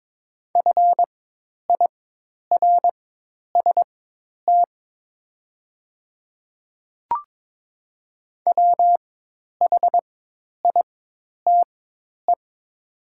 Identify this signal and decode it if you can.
Morse code, heard as a keyed tone, about 22 wpm, spaced as for 10 wpm, 700 Hz. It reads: FIRST WHITE
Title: 4x Character Spacing - Top 300 Words - 22wpm